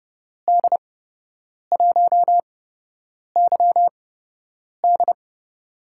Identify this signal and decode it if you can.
Morse code, heard as a keyed tone, about 30 wpm, spaced as for 8 wpm, 700 Hz. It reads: D1YD
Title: USA Call Signs - 5 Characters - 8x Character Spacing - 30wpm